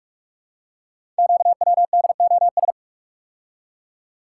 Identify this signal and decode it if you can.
Morse code, heard as a keyed tone, about 44 wpm, 690 Hz. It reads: QWDOS
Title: Code Groups - 5 Letters - 45wpm